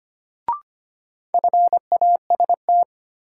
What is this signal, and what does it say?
Morse code, heard as a keyed tone, about 25 wpm, 700 Hz. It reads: FAST
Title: Sentences from Top 400 Words - New Word Drill - 25wpm